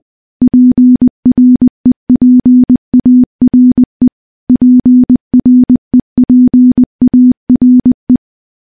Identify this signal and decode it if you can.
Morse code, heard as a keyed tone, about 20 wpm, 250 Hz. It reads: PREPARE PREPARE
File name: morse-training-20WPM